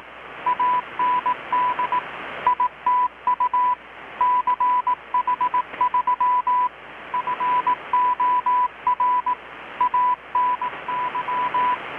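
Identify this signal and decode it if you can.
Morse code, heard as a keyed tone, about 18 wpm, 990 Hz, with radio noise. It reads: AND ITU CH3 FOR ANY
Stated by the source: Morse from KPH's Night of Nights XX maritime radiotelegraphy event, received in Chiapas, Mexico about 3,800 km away on a Kenwood R-600 with a grounded 10 m long-wire antenna.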